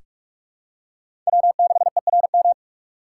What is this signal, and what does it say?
Morse code, heard as a keyed tone, about 46 wpm, 700 Hz. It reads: W6ERM
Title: USA Call Signs - 5 Characters - 2x Character Spacing - 45wpm